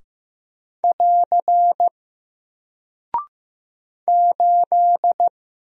Morse code, heard as a keyed tone, about 15 wpm, 700 Hz. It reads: <AR> 8
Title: Single Character (Letter:Number:Punctuation:Prosign) Alt 15wpm